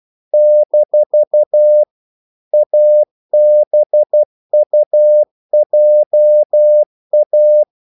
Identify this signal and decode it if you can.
Morse code, heard as a keyed tone, about 12 wpm, 600 Hz. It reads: - ABUJA
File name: CW / Practice / NationalCapitals12